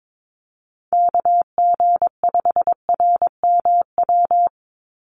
Morse code, heard as a keyed tone, about 22 wpm, 700 Hz. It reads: KG5RMW
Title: USA Call Signs 22wpm